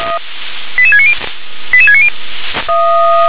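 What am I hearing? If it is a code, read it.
Five-tone selective call (ZVEI), heard as quick runs of five tones, tones of about 70 ms. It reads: 88690 88690
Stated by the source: ZVEI five-tone selective call received via SDR.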